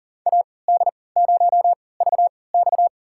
Morse code, heard as a keyed tone, about 40 wpm, spaced as for 23 wpm, 700 Hz. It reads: AD0VX
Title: USA Call Signs - 5 Characters - 3x Character Spacing - 40wpm